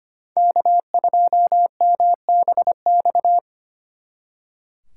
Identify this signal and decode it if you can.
Morse code, heard as a keyed tone, about 29 wpm, 700 Hz. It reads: K2MBX
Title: Speed Racing - USA 1x3 Call Signs 25wpm